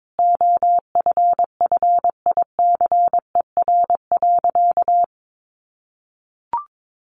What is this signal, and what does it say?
Morse code, heard as a keyed tone, about 22 wpm, 700 Hz. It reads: OFFICER.
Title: Sentences from Top 1200 Words 22wpm